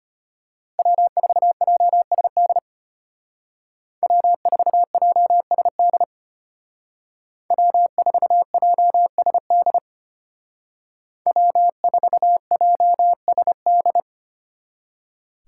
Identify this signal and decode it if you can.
Morse code, heard as a keyed tone, about 31 wpm, 700 Hz. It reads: W4JSD W4JSD W4JSD W4JSD
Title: Speed Racing - USA 1x3 Call Signs 25wpm